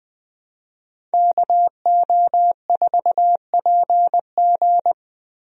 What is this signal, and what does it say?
Morse code, heard as a keyed tone, about 20 wpm, 700 Hz. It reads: KO4PG
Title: USA Call Signs - 5 Characters - Repeated 3x - 20wpm